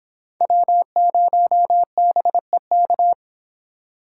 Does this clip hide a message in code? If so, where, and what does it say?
Yes: Morse code from 0.4 to 3.1 s, heard as a keyed tone, about 25 wpm, 700 Hz. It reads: W0BEK